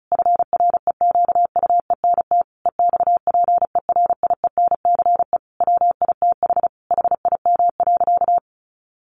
Morse code, heard as a keyed tone, about 34 wpm, 700 Hz. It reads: FREQUENT EXPERIENCE WITH HIM.
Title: Sentences from Top 1400 Words 35wpm